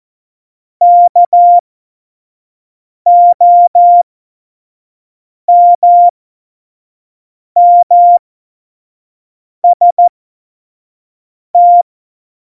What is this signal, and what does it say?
Morse code, heard as a keyed tone, about 14 wpm, spaced as for 5 wpm, 700 Hz. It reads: KOMMST
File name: Morsekurs / Witze / Witz 04. September 2018, 25 F70